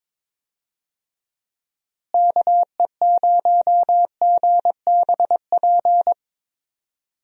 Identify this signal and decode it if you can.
Morse code, heard as a keyed tone, about 22 wpm, 700 Hz. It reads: KE0GBP
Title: USA Call Signs - 6 Characters - Repeated 3x - 22wpm